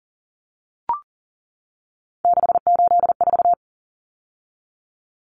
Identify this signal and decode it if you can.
Morse code, heard as a keyed tone, about 40 wpm, 700 Hz. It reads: 684
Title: Numbers - 3-Digit 40wpm